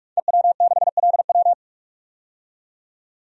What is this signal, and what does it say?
Morse code, heard as a keyed tone, about 46 wpm, 700 Hz. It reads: EW6LW
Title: DXCC-100 Call Signs - 5 Characters - 45wpm